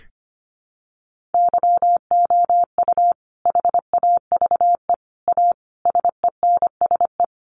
Morse code, heard as a keyed tone, about 25 wpm, 700 Hz. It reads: YOU HAVE A SENSE
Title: Sentences from Top 700 Words 25wpm